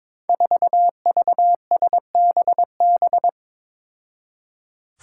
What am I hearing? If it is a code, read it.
Morse code, heard as a keyed tone, about 22 wpm, 700 Hz. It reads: 4VSBB
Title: Five Challenging Letter-Number Combinations 22wpm